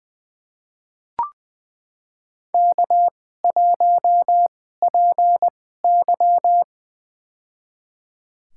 Morse code, heard as a keyed tone, about 20 wpm, spaced as for 14 wpm, 700 Hz. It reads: K1PY
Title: USA Call Signs - 2x Character Spacing - 20wpm